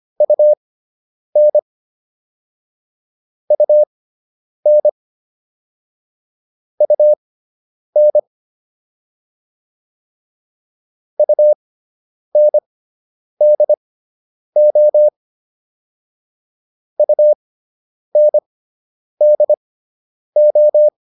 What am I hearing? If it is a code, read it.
Morse code, heard as a keyed tone, about 25 wpm, spaced as for 9 wpm, 600 Hz. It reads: UN UN UN UNDO UNDO